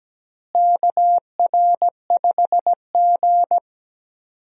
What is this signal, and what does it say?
Morse code, heard as a keyed tone, about 17 wpm, 700 Hz. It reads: KR5G
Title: USA Call Signs - 4 Characters - Repeated 2x - 17wpm